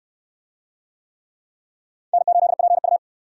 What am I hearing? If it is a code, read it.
Morse code, heard as a keyed tone, about 67 wpm, 700 Hz. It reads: N8PU